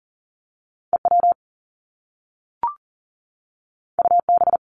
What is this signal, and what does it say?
Morse code, heard as a keyed tone, about 39 wpm, 700 Hz. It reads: EW UB
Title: Top Two Letter Combinations in Words 40wpm